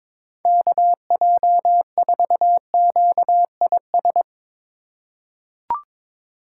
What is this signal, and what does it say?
Morse code, heard as a keyed tone, about 22 wpm, 700 Hz. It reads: KJ4QIS
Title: USA Call Signs - 6 Characters - 2x Character Spacing - 22wpm